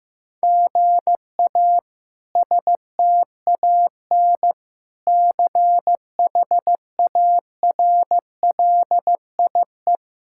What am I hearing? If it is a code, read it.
Morse code, heard as a keyed tone, about 15 wpm, 700 Hz. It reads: GA STAN CHARLIE
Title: K1USN SST QSO Practice - 15wpm